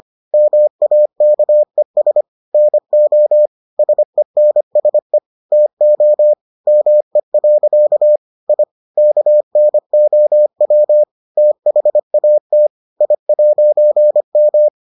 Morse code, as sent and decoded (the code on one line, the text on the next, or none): -- .- -.- . ... / -. --- / ... -. ... / - --- / -- . .-.-.- / .. / -.- -. --- .-- / - .... .- - / .. .----. --
MAKES NO SENSE TO ME. I KNOW THAT I'M